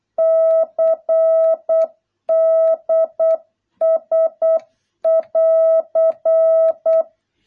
Morse code, heard as a keyed tone, about 8 wpm, 650 Hz. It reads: CDS <AR>